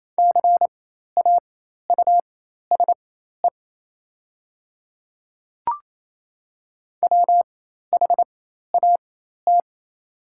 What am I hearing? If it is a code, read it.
Morse code, heard as a keyed tone, about 28 wpm, spaced as for 13 wpm, 700 Hz. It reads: CAUSE WHAT
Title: 4x Character Spacing - Top 300 Words - 28wpm